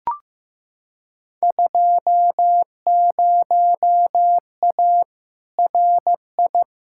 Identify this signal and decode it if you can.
Morse code, heard as a keyed tone, about 15 wpm, 700 Hz. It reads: 20A RI
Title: Field Day Exchanges Encore - 15wpm